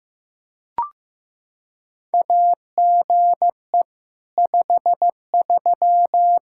Morse code, heard as a keyed tone, about 15 wpm, 700 Hz. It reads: AGE 53
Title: Practice QSO - Exchange 3 - More Info - 15wpm